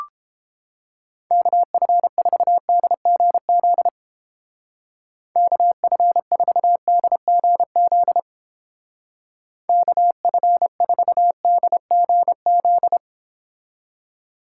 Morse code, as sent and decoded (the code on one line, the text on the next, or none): -.- ..-. ....- -.. --. --.. / -.- ..-. ....- -.. --. --.. / -.- ..-. ....- -.. --. --..
KF4DGZ KF4DGZ KF4DGZ